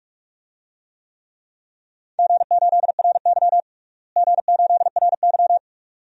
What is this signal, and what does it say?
Morse code, heard as a keyed tone, about 43 wpm, 700 Hz. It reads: G8RY G8RY
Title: DXCC-100 Call Signs - 4 Characters - Repeated 2x - 45wpm